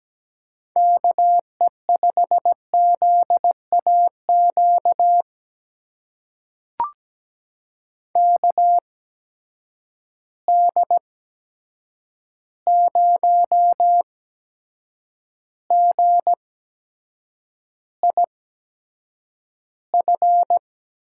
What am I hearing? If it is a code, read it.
Morse code, heard as a keyed tone, about 17 wpm, spaced as for 7 wpm, 700 Hz. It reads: KE5ZAQ KD0GIF
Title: USA Call Signs - 6 Characters - 8x Character Spacing - 17wpm